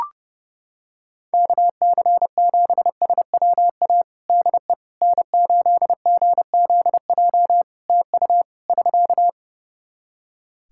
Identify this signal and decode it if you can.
Morse code, heard as a keyed tone, about 30 wpm, 700 Hz. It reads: KC7SWA DE N8GZJ TU <SK>
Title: Practice QSO - Exchange 4 - Ending - 30wpm